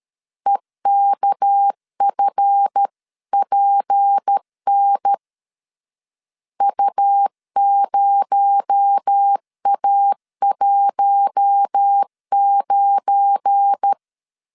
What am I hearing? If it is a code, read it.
Morse code, heard as a keyed tone, about 13 wpm, 800 Hz. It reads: EKFPN U0A19
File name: CW / DET / 1 9 1500C S A 12wpm 3800